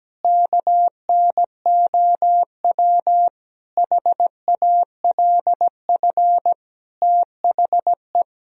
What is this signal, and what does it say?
Morse code, heard as a keyed tone, about 17 wpm, 700 Hz. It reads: KNOW HALF THE
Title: Sentences from Top 1000 Words - Review Part 1 - 17wpm